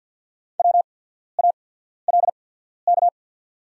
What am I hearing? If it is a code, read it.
Morse code, heard as a keyed tone, about 50 wpm, spaced as for 14 wpm, 700 Hz. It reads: WALK
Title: 8x Character Spacing - Top 300 Words - 50wpm